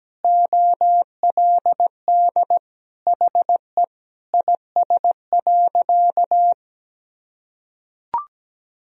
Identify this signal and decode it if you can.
Morse code, heard as a keyed tone, about 17 wpm, 700 Hz. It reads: OLD HE IS.
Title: Sentences from Top 1000 Words - Review Part 4 - 17wpm